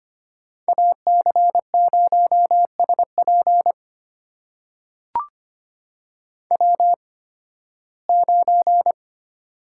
Morse code, heard as a keyed tone, about 25 wpm, spaced as for 16 wpm, 700 Hz. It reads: AC0SP W9
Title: USA Call Signs - 5 Characters - 8x Character Spacing - 25wpm